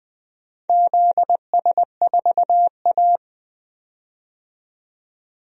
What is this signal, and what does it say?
Morse code, heard as a keyed tone, about 20 wpm, 700 Hz. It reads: ZS4A
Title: DXCC-100 Call Signs - 4 Characters - 20wpm